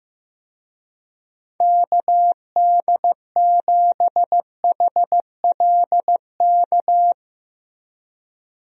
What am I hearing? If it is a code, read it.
Morse code, heard as a keyed tone, about 15 wpm, 700 Hz. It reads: KD7HLK
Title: USA Call Signs - 6 Characters - Repeated 2x - 15wpm